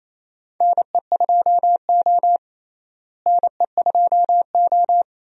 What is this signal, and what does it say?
Morse code, heard as a keyed tone, about 28 wpm, 700 Hz. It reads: NE2O NE2O
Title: USA Call Signs - 4 Characters - Repeated 3x - 28wpm